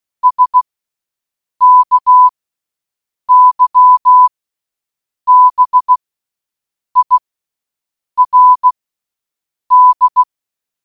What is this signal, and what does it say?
Morse code, heard as a keyed tone, about 16 wpm, spaced as for 7 wpm, 1000 Hz. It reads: SKYBIRD